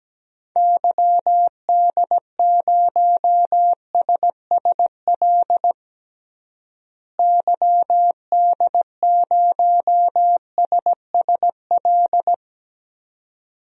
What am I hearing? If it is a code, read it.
Morse code, heard as a keyed tone, about 17 wpm, 700 Hz. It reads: YD0SSL YD0SSL
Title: DXCC-100 Call Signs - 6 Characters - Repeated 2x - 17wpm